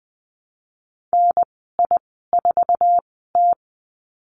Morse code, heard as a keyed tone, about 20 wpm, spaced as for 14 wpm, 700 Hz. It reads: NI4T
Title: 062 Lesson 11g - Letter h - Call Signs 20wpm